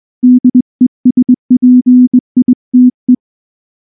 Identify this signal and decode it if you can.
Morse code, heard as a keyed tone, about 19 wpm, 250 Hz. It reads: DESPITE
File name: morse-training-20WPM